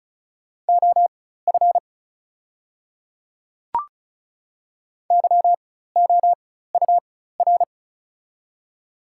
Morse code, heard as a keyed tone, about 36 wpm, spaced as for 16 wpm, 690 Hz. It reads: OF YOUR
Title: 4x Character Spacing - Top 100 Words - 35wpm